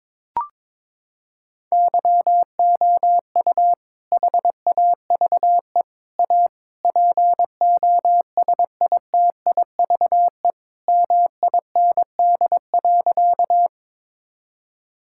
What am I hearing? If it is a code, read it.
Morse code, heard as a keyed tone, about 22 wpm, 700 Hz. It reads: YOU HAVE A POSITIVE MIND.
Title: Sentences from Top 1000 Words - Review Part 5 - 22wpm